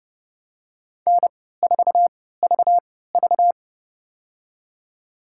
Morse code, heard as a keyed tone, about 30 wpm, spaced as for 17 wpm, 700 Hz. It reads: N4VV
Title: USA Call Signs - 4 Characters - 3x Character Spacing - 30wpm